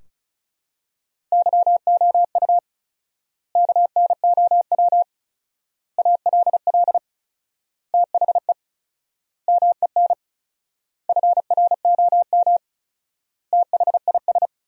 Morse code, heard as a keyed tone, about 36 wpm, 700 Hz. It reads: YOU KNOW ALL THE MEN FROM THIS
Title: 4x Word-Spacing - Sentences from Top 300 Words - 35wpm